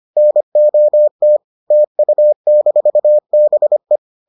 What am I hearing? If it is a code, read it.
Morse code, heard as a keyed tone, about 25 wpm, 600 Hz. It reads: NOT TU-BE